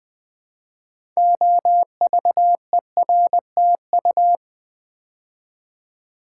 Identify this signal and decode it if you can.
Morse code, heard as a keyed tone, about 20 wpm, 700 Hz. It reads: OVERTU
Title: Build A Word - 3 Letters At A Time - 8-Letter Words - 20wpm